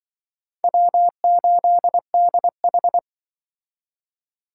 Morse code, heard as a keyed tone, about 23 wpm, 700 Hz. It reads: W8DH
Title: Speed Racing - USA Call Signs 20wpm